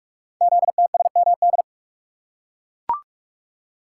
Morse code, heard as a keyed tone, about 44 wpm, 700 Hz. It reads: ZTSMD
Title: Code Groups - 5 Letters - 45wpm